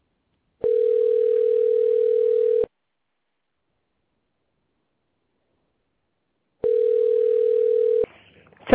Ringback tone (North America), 440 + 480 Hz.